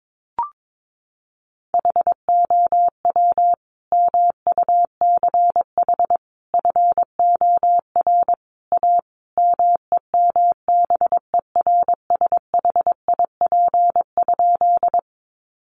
Morse code, heard as a keyed tone, about 22 wpm, 700 Hz. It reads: HOW MUCH FOR A MEMBERSHIP?
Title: Sentences from Top 1400 Words 22wpm